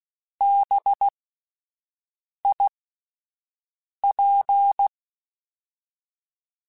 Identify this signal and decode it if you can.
Morse code, heard as a keyed tone, about 16 wpm, spaced as for 5 wpm, 800 Hz. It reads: BIP
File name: Alphabetical / Alphabetical mp3s / Bs @5